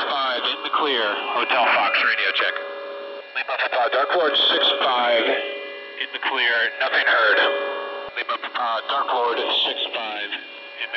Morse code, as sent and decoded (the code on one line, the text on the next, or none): none